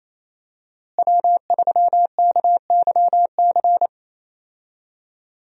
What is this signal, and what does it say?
Morse code, heard as a keyed tone, about 28 wpm, 700 Hz. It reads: W3KYC
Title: USA Call Signs - 5 Characters - Repeated 2x - 28wpm